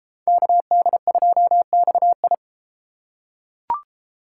Morse code, heard as a keyed tone, about 33 wpm, 690 Hz. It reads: KD2XI